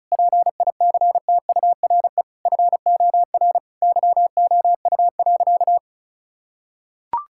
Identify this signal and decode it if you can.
Morse code, heard as a keyed tone, about 36 wpm, 700 Hz. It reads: PICTURE FOR YOU.
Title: Sentences from Top 1000 Words - Review Part 6 - 35wpm